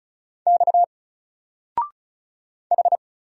Morse code, heard as a keyed tone, about 33 wpm, 700 Hz. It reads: X H
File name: Single-Letter-Number-35wpm